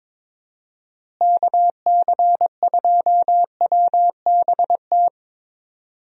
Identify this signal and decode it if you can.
Morse code, heard as a keyed tone, about 22 wpm, 700 Hz. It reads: KC2WBT
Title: USA Call Signs - 6 Characters - Repeated 3x - 22wpm